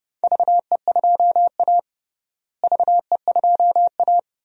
Morse code, heard as a keyed tone, about 31 wpm, 700 Hz. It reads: VE2A VE2A
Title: DXCC-100 Call Signs - 4 Characters - Repeated 3x - 30wpm